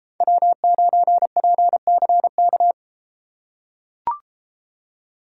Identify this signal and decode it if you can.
Morse code, heard as a keyed tone, about 32 wpm, 700 Hz. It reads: W9PCK